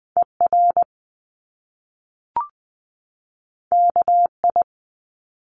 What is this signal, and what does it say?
Morse code, heard as a keyed tone, about 19 wpm, 700 Hz. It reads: ER KI
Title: Top Two Letter Combinations in Words 20wpm